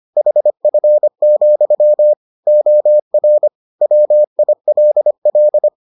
Morse code, heard as a keyed tone, about 25 wpm, 600 Hz. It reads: HF, OR WILL